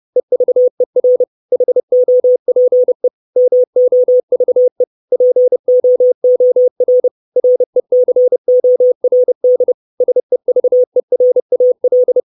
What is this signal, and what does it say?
Morse code, heard as a keyed tone, about 30 wpm, 500 Hz. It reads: EVER HOPE MOVE POOR RECORD SEVERAL
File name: 632 Words 30WPM